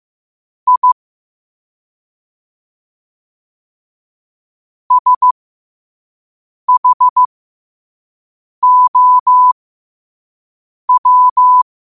Morse code, heard as a keyed tone, about 15 wpm, spaced as for 5 wpm, 1000 Hz. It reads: I SHOW
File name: qotdfiles / 5wpm